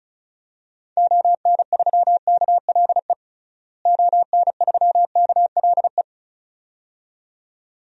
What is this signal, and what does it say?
Morse code, heard as a keyed tone, about 36 wpm, 700 Hz. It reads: ON3KLE ON3KLE
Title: DXCC-100 Call Signs - 6 Characters - Repeated 2x - 35wpm